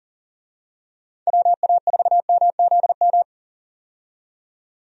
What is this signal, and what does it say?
Morse code, heard as a keyed tone, about 40 wpm, 700 Hz. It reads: WA4MZM
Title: USA Call Signs - 6 Characters - 40wpm